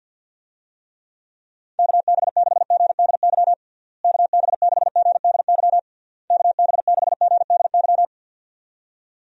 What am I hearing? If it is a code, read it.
Morse code, heard as a keyed tone, about 52 wpm, 700 Hz. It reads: KB6GDY KB6GDY KB6GDY
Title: USA Call Signs - 6 Characters - Repeated 3x - 50wpm